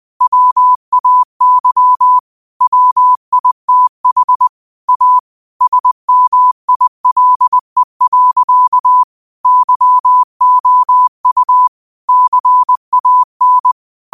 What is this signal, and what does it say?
Morse code, heard as a keyed tone, about 20 wpm, 1000 Hz. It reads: WAY WITH A SMILE. YOU CAN